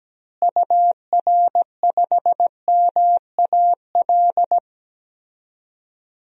Morse code, heard as a keyed tone, about 17 wpm, 700 Hz. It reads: UR5MAL
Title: DXCC-100 Call Signs - 6 Characters - Repeated 3x - 17wpm